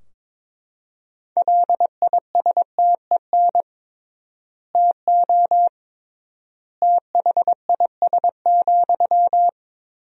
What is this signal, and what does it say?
Morse code, heard as a keyed tone, about 22 wpm, 700 Hz. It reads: LISTEN TO THIS,
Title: 3x Word-Spacing - Sentences from Top 400 Words - 22wpm